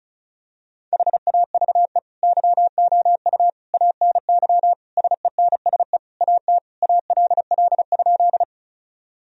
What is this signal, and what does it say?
Morse code, heard as a keyed tone, about 36 wpm, 700 Hz. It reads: HAVE YOU ANY SENSE AT ALL?
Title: Sentences from Top 1000 Words - Review Part 4 - 35wpm